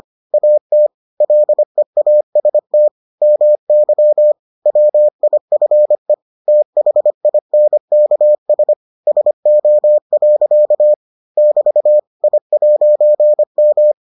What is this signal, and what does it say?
Morse code, heard as a keyed tone, about 25 wpm, 600 Hz. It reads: AT LEAST MY WIFE THINKS SO. <BT> I'M